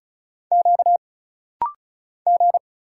Morse code, heard as a keyed tone, about 36 wpm, 700 Hz. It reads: Q G
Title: Single Letters - Rapid-Fire 35wpm